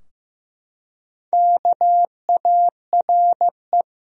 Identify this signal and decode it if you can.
Morse code, heard as a keyed tone, about 16 wpm, 700 Hz. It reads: KARE
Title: USA Common Names - 15wpm